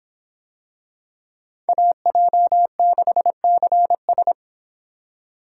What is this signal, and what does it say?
Morse code, heard as a keyed tone, about 23 wpm, 700 Hz. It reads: AJ6CS